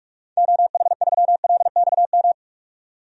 Morse code, heard as a keyed tone, about 46 wpm, 700 Hz. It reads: OH3LXM
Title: DXCC-100 Call Signs - 6 Characters - 45wpm